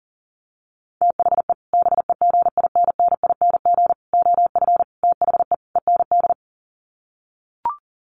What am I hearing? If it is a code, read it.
Morse code, heard as a keyed tone, about 40 wpm, 700 Hz. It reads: THE BEGINNING OF THE END
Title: Sets of 5 Words Encore2 40wpm